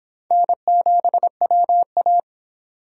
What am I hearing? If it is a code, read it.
Morse code, heard as a keyed tone, about 25 wpm, 700 Hz. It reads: N7WA